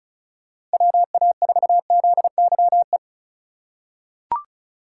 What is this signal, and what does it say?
Morse code, heard as a keyed tone, about 34 wpm, 700 Hz. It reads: WA4ZYE
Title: USA Call Signs - 6 Characters - 4x Character Spacing - 35wpm